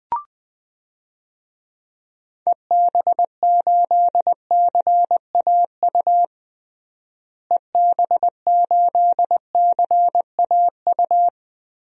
Morse code, heard as a keyed tone, about 21 wpm, 700 Hz. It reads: EB8CAU EB8CAU
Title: DXCC-100 Call Signs - 6 Characters - Repeated 3x - 20wpm